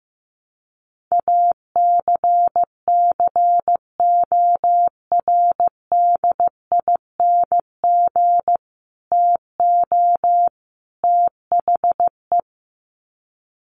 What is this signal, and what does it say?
Morse code, heard as a keyed tone, about 14 wpm, 700 Hz. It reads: ACCORDING TO THE
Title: Sets of 3 Words 15wpm